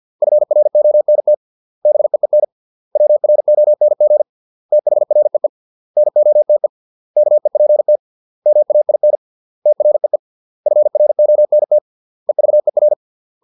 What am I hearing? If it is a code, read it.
Morse code, heard as a keyed tone, about 50 wpm, 600 Hz. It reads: FRONT BEEN WRONG THREE NOTE KEPT MAIN TREE FRONT EVER